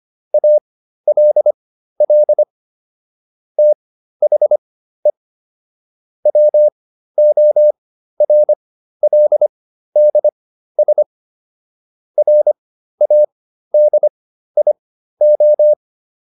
Morse code, heard as a keyed tone, about 25 wpm, spaced as for 13 wpm, 600 Hz. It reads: ALL THE WORLDS RADIO